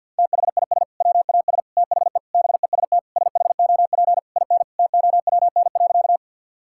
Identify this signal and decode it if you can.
Morse code, heard as a keyed tone, about 50 wpm, 700 Hz. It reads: THIS WAS THE BEST SHOP IN TOWN.
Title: Sentences from Top 1600 Words 50wpm